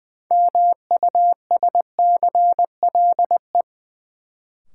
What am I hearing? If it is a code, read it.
Morse code, heard as a keyed tone, about 20 wpm, 700 Hz. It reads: MUSCLE
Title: Sentences from Top 1300 Words - New Word Drill - 20wpm